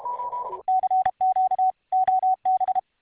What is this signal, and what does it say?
Morse code, heard as a keyed tone, about 31 wpm, 750 Hz. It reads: CQ OB